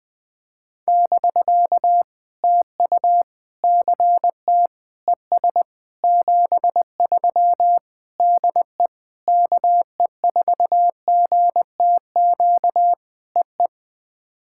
Morse code, heard as a keyed tone, about 20 wpm, 700 Hz. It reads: <BK> TU CT ES 73 DE KE4GTQ EE
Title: POTA QSO Practice - 20wpm